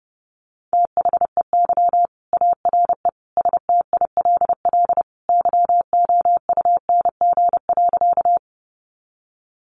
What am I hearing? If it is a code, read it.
Morse code, heard as a keyed tone, about 30 wpm, 700 Hz. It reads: THEY ARE STILL YOUNG.
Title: Sentences from Top 400 Words 30wpm